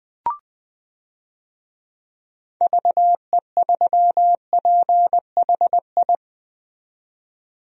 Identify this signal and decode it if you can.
Morse code, heard as a keyed tone, about 20 wpm, 700 Hz. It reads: VE3PHI
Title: DXCC-100 Call Signs - 6 Characters - 20wpm